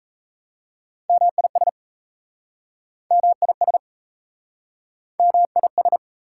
Morse code, heard as a keyed tone, about 38 wpm, 690 Hz. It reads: MIS MIS MIS